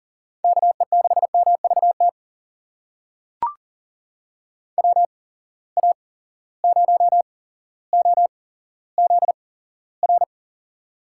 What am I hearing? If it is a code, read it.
Morse code, heard as a keyed tone, about 40 wpm, spaced as for 17 wpm, 700 Hz. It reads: KE6MVT WA0OZR
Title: USA Call Signs - 6 Characters - 8x Character Spacing - 40wpm